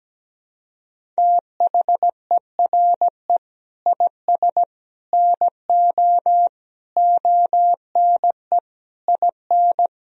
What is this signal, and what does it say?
Morse code, heard as a keyed tone, about 17 wpm, 700 Hz. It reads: THERE IS NO ONE IN
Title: Sentences from Top 500 Words and Abbreviations 17wpm